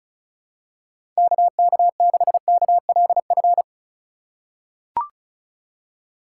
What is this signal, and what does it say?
Morse code, heard as a keyed tone, about 36 wpm, 700 Hz. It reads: KK6KLF
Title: USA Call Signs - 6 Characters - 4x Character Spacing - 35wpm